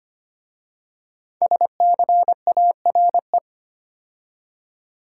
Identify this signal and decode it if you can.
Morse code, heard as a keyed tone, about 25 wpm, 700 Hz. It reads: SCARE
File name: 5 Letter Words - First 3 Letters Given - 25wpm